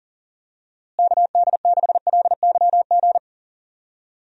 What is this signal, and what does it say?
Morse code, heard as a keyed tone, about 40 wpm, 700 Hz. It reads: KD6LYG
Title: USA Call Signs - 6 Characters - Repeated 2x - 40wpm